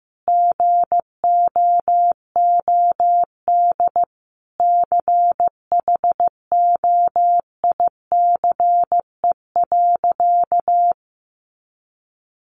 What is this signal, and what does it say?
Morse code, heard as a keyed tone, about 15 wpm, 700 Hz. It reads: GOOD CHOICE.
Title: Sentences from Top 900 Words 15wpm